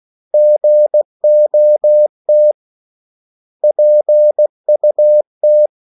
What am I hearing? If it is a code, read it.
Morse code, heard as a keyed tone, about 16 wpm, 600 Hz. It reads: GOT PUT
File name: CW / Practice / 500MostCommonEnglishWords16